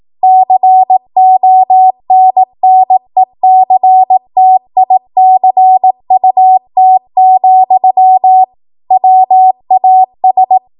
Morse code, heard as a keyed tone, about 18 wpm, 750 Hz. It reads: CONNECTICUT, WAS